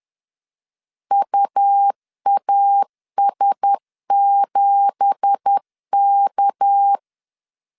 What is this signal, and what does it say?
Morse code, heard as a keyed tone, about 10 wpm, 800 Hz. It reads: UAS7K
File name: CW / DET / 1 9 1500C S B 10wpm 4439